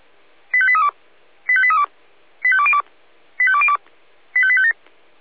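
Five-tone selective call (ZVEI), heard as quick runs of five tones, tones of about 70 ms. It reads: 85832 85832 85202 85202 85857
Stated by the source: ZVEI five-tone selective call 85254 received on SDR.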